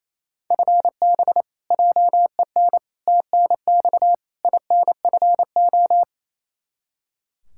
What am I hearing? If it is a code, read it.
Morse code, heard as a keyed tone, about 28 wpm, 700 Hz. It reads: FB JEN TNX INFO
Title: Practice QSO - Exchange 3 - More Info - 28wpm